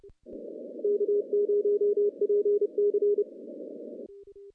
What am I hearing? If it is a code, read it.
Morse code, heard as a keyed tone, about 28 wpm, 410 Hz. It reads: K0PC
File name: contests / CW-SS-k5tr- -1400